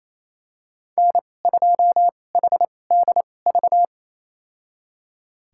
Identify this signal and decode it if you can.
Morse code, heard as a keyed tone, about 28 wpm, spaced as for 20 wpm, 700 Hz. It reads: N2HDV
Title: USA Call Signs - 5 Characters - 2x Character Spacing - 28wpm